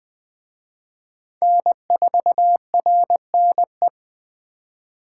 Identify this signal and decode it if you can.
Morse code, heard as a keyed tone, about 20 wpm, 700 Hz. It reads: N4RNE
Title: Speed Racing - USA 1x3 Call Signs 17wpm